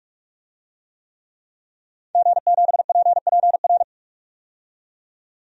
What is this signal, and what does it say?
Morse code, heard as a keyed tone, about 46 wpm, 700 Hz. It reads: G7PPR
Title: DXCC-100 Call Signs - 5 Characters - 45wpm